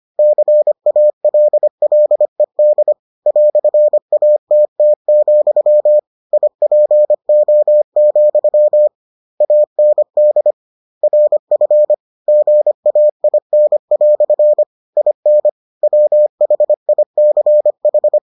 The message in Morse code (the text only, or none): CALLED "ATT, IPO, AND RF GAIN" IN WHICH